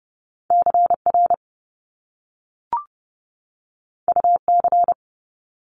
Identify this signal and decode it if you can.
Morse code, heard as a keyed tone, about 30 wpm, 700 Hz. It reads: CR UC
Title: Top Two Letter Combinations in Words 30wpm